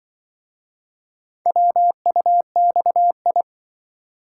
Morse code, heard as a keyed tone, about 25 wpm, 700 Hz. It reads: WUXI